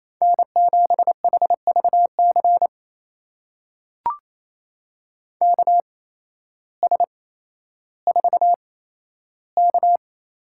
Morse code, heard as a keyed tone, about 28 wpm, spaced as for 13 wpm, 700 Hz. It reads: N7HVC KS4K
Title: USA Call Signs - 5 Characters - 8x Character Spacing - 28wpm